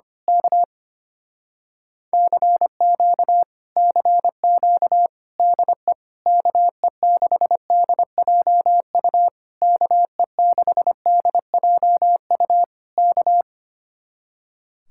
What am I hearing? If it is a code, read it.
Morse code, heard as a keyed tone, about 24 wpm, 700 Hz. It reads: K CQ CQ DE KE6DJU KE6DJU K
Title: Speed-Racing Common QSO Phrases - 25wpm